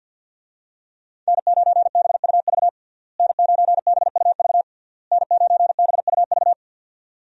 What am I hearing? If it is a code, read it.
Morse code, heard as a keyed tone, about 49 wpm, 700 Hz. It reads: N9BUV N9BUV N9BUV
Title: USA Call Signs - 5 Characters - Repeated 3x - 50wpm